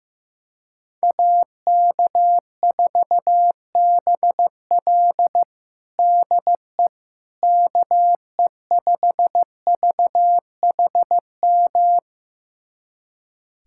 Morse code, heard as a keyed tone, about 15 wpm, 700 Hz. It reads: AK4BL DE KE5VHM
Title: USA Callsign DE Callsign 15wpm